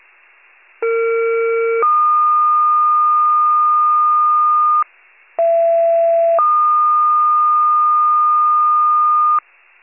Two-tone page, heard as a one-second tone, then a three-second tone, about 460 then 1220 Hz, about 670 then 1220 Hz. It is heard from 0.8 to 9.4 s.